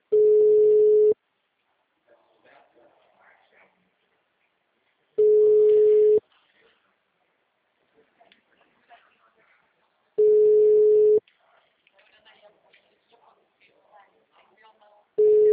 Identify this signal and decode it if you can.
Ringback tone (Europe, 425 Hz), with speech.